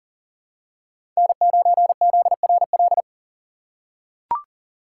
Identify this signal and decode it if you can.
Morse code, heard as a keyed tone, about 40 wpm, 700 Hz. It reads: N9ZRL
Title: USA Call Signs - 5 Characters - 4x Character Spacing - 40wpm